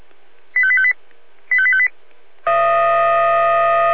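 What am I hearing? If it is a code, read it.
Five-tone selective call (ZVEI), heard as quick runs of five tones, tones of about 70 ms. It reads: 85858 85858
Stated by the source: ZVEI five-tone selective call received on SDR.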